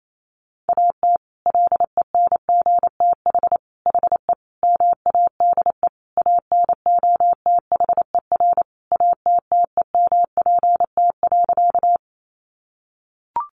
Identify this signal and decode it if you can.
Morse code, heard as a keyed tone, about 28 wpm, 700 Hz. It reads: AT LENGTH HE MADE ANOTHER ATTEMPT.
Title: Sentences from Top 1000 Words 28wpm